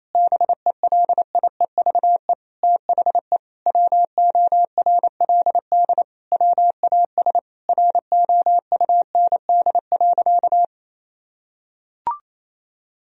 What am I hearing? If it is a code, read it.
Morse code, heard as a keyed tone, about 28 wpm, 700 Hz. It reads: BELIEVE THE WORLD WAS ROUND.
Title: Sentences from Top 1000 Words - Review Part 1 - 28wpm